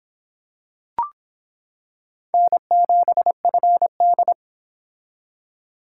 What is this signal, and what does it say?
Morse code, heard as a keyed tone, about 25 wpm, 700 Hz. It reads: N7FD